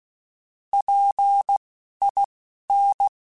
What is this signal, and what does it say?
Morse code, heard as a keyed tone, about 16 wpm, spaced as for 11 wpm, 800 Hz. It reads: PIN
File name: Alphabetical / Alphabetical mp3s / Ps @10